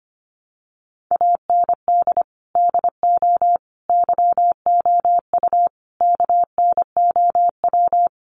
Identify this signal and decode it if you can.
Morse code, heard as a keyed tone, about 25 wpm, 700 Hz. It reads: AND DO YOU KNOW
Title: Sentences from Top 200 Words 25wpm